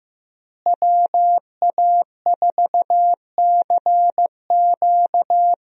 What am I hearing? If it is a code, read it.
Morse code, heard as a keyed tone, about 15 wpm, 700 Hz. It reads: WA4CQ
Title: USA Call Signs - 5 Characters - Repeated 2x - 15wpm